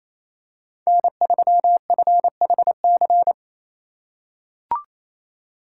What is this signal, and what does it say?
Morse code, heard as a keyed tone, about 28 wpm, 700 Hz. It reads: N3FHC
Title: USA Call Signs - 5 Characters - 4x Character Spacing - 28wpm